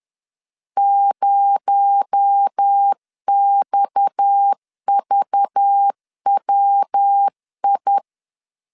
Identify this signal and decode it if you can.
Morse code, heard as a keyed tone, about 10 wpm, 800 Hz. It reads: 0XVWI